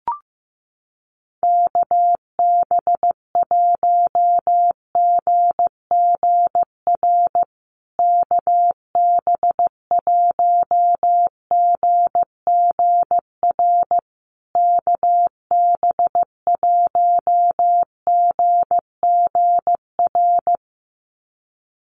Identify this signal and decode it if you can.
Morse code, heard as a keyed tone, about 15 wpm, 700 Hz. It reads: KB1GGR KB1GGR KB1GGR
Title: USA Call Signs Repeated 3x 15wpm